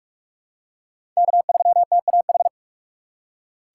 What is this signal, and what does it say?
Morse code, heard as a keyed tone, about 46 wpm, 700 Hz. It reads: K3TAH
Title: USA Call Signs - 3x Character Spacing - 45wpm